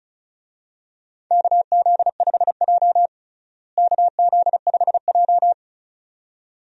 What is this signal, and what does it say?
Morse code, heard as a keyed tone, about 36 wpm, 700 Hz. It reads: KZ5J KZ5J
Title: USA Call Signs - 4 Characters - Repeated 2x - 35wpm